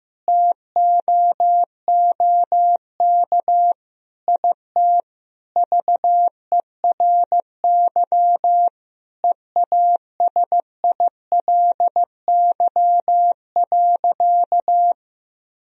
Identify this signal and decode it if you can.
Morse code, heard as a keyed tone, about 15 wpm, 700 Hz. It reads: TOOK IT VERY EASILY.
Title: Sentences from Top 1000 Words - Review Part 1 - 15wpm